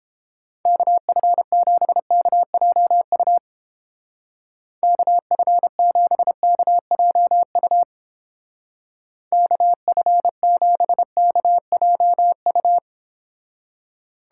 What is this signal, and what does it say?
Morse code, heard as a keyed tone, about 30 wpm, 700 Hz. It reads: KF7KJU KF7KJU KF7KJU